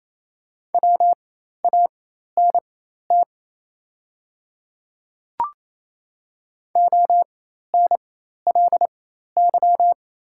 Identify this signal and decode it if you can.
Morse code, heard as a keyed tone, about 28 wpm, spaced as for 13 wpm, 690 Hz. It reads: WANT ONLY
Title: 4x Character Spacing - Top 200 Words - 28wpm